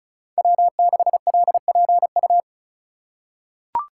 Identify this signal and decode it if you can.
Morse code, heard as a keyed tone, about 36 wpm, 700 Hz. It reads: W6LPU
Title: USA Call Signs - 3x Character Spacing - 35wpm